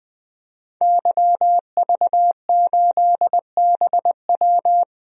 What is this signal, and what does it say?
Morse code, heard as a keyed tone, about 20 wpm, 700 Hz. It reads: YV8BW
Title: DXCC-100 Call Signs - 5 Characters - Repeated 2x - 20wpm